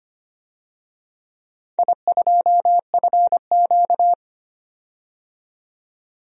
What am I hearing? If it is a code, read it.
Morse code, heard as a keyed tone, about 25 wpm, 700 Hz. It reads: I2FQ
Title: DXCC-100 Call Signs - 4 Characters - 25wpm